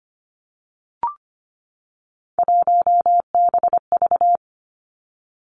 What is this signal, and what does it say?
Morse code, heard as a keyed tone, about 25 wpm, 700 Hz. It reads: 1BV
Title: Three Challenging Letter-Number Combinations 25wpm